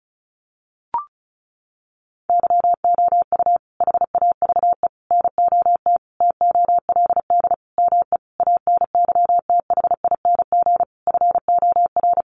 Morse code, heard as a keyed tone, about 34 wpm, 700 Hz. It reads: YOU HAVE NOT TOLD ME ANYTHING FOR
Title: Sentences from Top 1000 Words 35wpm